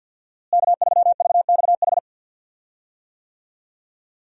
Morse code, heard as a keyed tone, about 47 wpm, 700 Hz. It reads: K3VXH
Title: USA Call Signs - 5 Characters - Repeated 2x - 50wpm